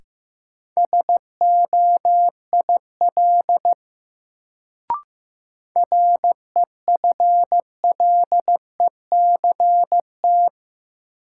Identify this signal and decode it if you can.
Morse code, heard as a keyed tone, about 15 wpm, 700 Hz. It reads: SOIL REFLECT
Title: Sentences from Top 1500 Words - New Word Drill Alt - 15wpm